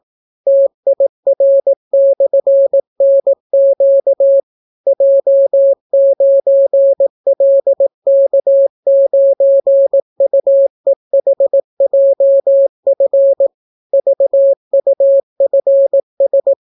Morse code, heard as a keyed tone, about 18 wpm, 550 Hz. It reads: TIR/NQ J9LK9UEHJF VUFS